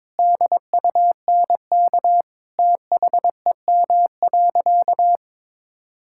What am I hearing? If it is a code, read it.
Morse code, heard as a keyed tone, about 22 wpm, 700 Hz. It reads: DUNK THEM.
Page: morse-code-ninja-practice-22wpm